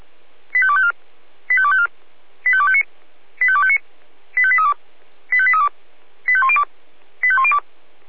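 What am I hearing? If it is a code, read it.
Five-tone selective call (ZVEI), heard as quick runs of five tones, tones of about 70 ms. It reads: 85254 85254 85259 85259 85832 85832 85102 85102
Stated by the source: ZVEI five-tone selective call 85248 received by SDR.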